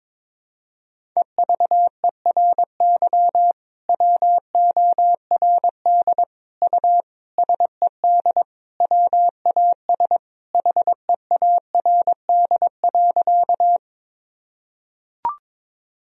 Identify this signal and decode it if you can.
Morse code, heard as a keyed tone, about 22 wpm, 700 Hz. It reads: EVERY WORD U SED WAS HEARD.
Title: Sentences from Top 400 Words and Abbreviations 22wpm